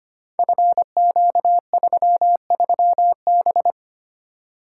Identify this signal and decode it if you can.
Morse code, heard as a keyed tone, about 25 wpm, 700 Hz. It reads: FQ33B
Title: Code Groups - 5 Letter:Numbers - 25wpm